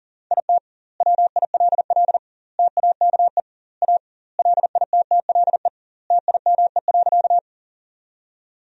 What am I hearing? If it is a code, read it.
Morse code, heard as a keyed tone, about 40 wpm, 700 Hz. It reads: IT WILL TAKE A LITTLE TIME.
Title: 2x Word-Spacing - Sentences from Top 200 Words - 40wpm